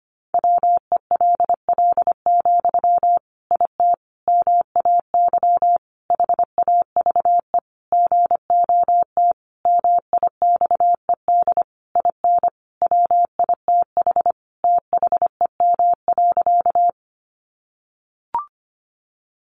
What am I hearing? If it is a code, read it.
Morse code, heard as a keyed tone, about 25 wpm, 700 Hz. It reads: WELL, IT MAY HAVE GOT MIXED IN WITH THEM.
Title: Sentences from Top 1500 Words 25wpm